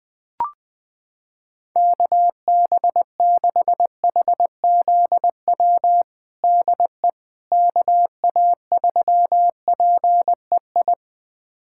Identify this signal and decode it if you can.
Morse code, heard as a keyed tone, about 21 wpm, 700 Hz. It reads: KB6HZW DE KA3PEI
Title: USA Callsign DE Callsign - 6 Characters - 20wpm